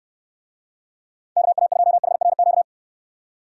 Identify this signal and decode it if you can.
Morse code, heard as a keyed tone, about 67 wpm, 700 Hz. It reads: KA2HRQ